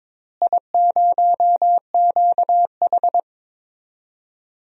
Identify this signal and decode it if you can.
Morse code, heard as a keyed tone, about 22 wpm, 700 Hz. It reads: I0QH
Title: DXCC-100 Call Signs - 4 Characters - Repeated 3x - 22wpm